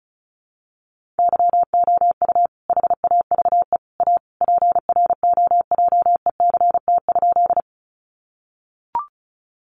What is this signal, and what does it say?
Morse code, heard as a keyed tone, about 34 wpm, 700 Hz. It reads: YOU HAVE A PROJECT?
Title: Sentences from Top 1400 Words 35wpm